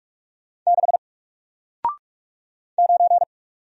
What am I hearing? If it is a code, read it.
Morse code, heard as a keyed tone, about 45 wpm, 700 Hz. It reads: 6 9
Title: Single Letter-Number 45wpm